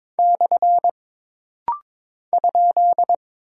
Morse code, heard as a keyed tone, about 22 wpm, 700 Hz. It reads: / ?
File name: practice / 22wpm / Single Punctuation-Prosign - Rapid-Fire - 22wpm